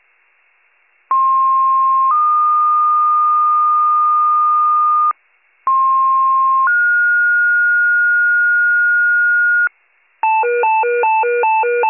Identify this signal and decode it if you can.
Two-tone page, heard as a one-second tone, then a three-second tone, about 1060 then 1220 Hz, about 1060 then 1430 Hz, with noise. It is heard from 1.1 to 9.7 s.